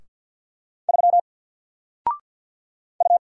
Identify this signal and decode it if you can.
Morse code, heard as a keyed tone, about 52 wpm, 700 Hz. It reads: 3 U
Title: Single Character (Letter:Number:Punctuation:Prosign) Alt 50wpm